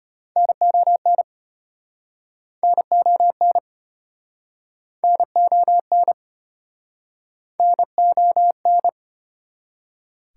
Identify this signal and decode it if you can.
Morse code, heard as a keyed tone, about 35 wpm, 690 Hz. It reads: NON NON NON NON